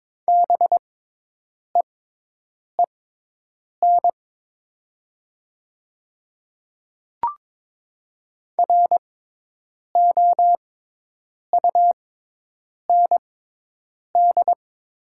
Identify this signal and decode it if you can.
Morse code, heard as a keyed tone, about 22 wpm, spaced as for 8 wpm, 700 Hz. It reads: BEEN ROUND
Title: 6x Character Spacing - Top 200 Words - 22wpm